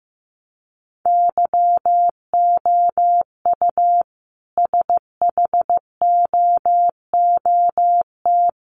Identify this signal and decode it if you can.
Morse code, heard as a keyed tone, about 15 wpm, 700 Hz. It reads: YOU SHOOT
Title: A Sound Of Thunder 15wpm